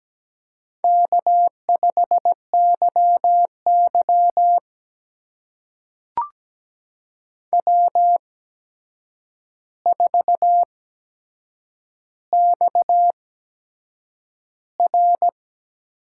Morse code, heard as a keyed tone, about 17 wpm, spaced as for 7 wpm, 700 Hz. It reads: K5YY W4XR
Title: USA Call Signs - 4 Characters - 8x Character Spacing - 17wpm